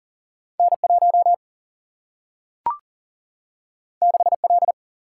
Morse code, heard as a keyed tone, about 40 wpm, 700 Hz. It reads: N 1 6 L
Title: Single-Letter Single-Number 40wpm